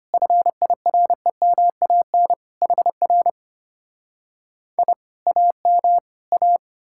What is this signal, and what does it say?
Morse code, heard as a keyed tone, about 33 wpm, 690 Hz. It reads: FIREMAN HR I AM A